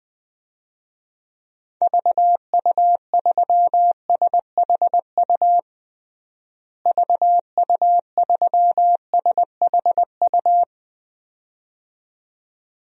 Morse code, heard as a keyed tone, about 21 wpm, 700 Hz. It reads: VU3SHU VU3SHU
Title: DXCC-100 Call Signs - 6 Characters - Repeated 2x - 20wpm